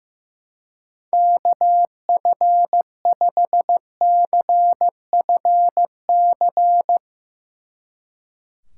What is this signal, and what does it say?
Morse code, heard as a keyed tone, about 17 wpm, 700 Hz. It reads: KF5CFC
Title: Speed Racing - USA 2x3 Call Signs 15wpm